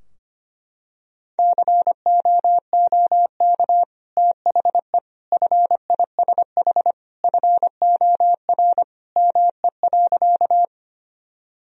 Morse code, heard as a keyed tone, about 25 wpm, 700 Hz. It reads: COOK THE FISH FOR ME.
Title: Sentences from Top 1000 Words - Review Part 4 - 25wpm